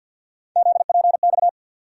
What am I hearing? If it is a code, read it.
Morse code, heard as a keyed tone, about 51 wpm, 700 Hz. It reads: ZPX
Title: USA Call Sign Postfixes - 3 Letters 50wpm